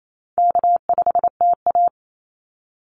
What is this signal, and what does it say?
Morse code, heard as a keyed tone, about 28 wpm, 700 Hz. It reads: K5TA
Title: Easy Call Signs - 28wpm